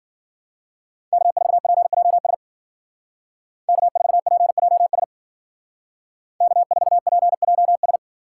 Morse code, heard as a keyed tone, about 53 wpm, 700 Hz. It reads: K4PJS K4PJS K4PJS